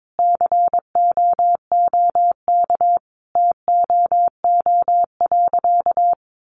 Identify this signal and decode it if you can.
Morse code, heard as a keyed tone, about 22 wpm, 700 Hz. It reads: COOK TOO.
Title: Sentences from Top 700 Words 22wpm